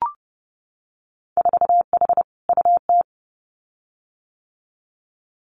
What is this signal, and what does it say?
Morse code, heard as a keyed tone, about 30 wpm, 700 Hz. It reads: 4H UT
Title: Winter Field Day Exchanges Encore 30wpm